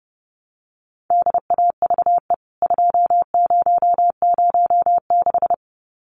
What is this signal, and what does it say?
Morse code, heard as a keyed tone, about 30 wpm, 700 Hz. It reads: DAVE 2006
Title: CWT Exchange 30wpm